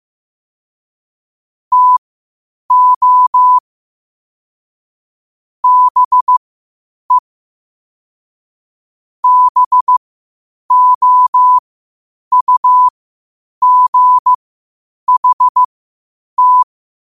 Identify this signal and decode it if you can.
Morse code, heard as a keyed tone, about 15 wpm, spaced as for 8 wpm, 1000 Hz. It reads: TO BE BOUGHT
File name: qotdfiles / 8wpm